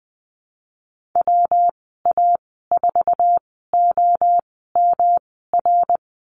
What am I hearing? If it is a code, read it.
Morse code, heard as a keyed tone, about 19 wpm, spaced as for 14 wpm, 700 Hz. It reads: WA4OMR
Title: 179 Lesson 29g - Slash - Call Signs 20wpm